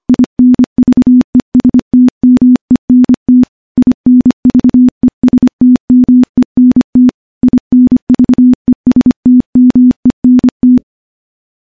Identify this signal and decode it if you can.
Morse code, heard as a keyed tone, about 25 wpm, 250 Hz. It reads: INVESTMENT INVESTMENT INVESTMENT
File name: morse / morse-training-25WPM